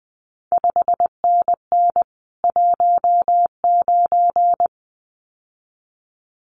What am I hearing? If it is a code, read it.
Morse code, heard as a keyed tone, about 20 wpm, 700 Hz. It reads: 5NN 19
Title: CQ WW DX Contest Exchange Encore - 20wpm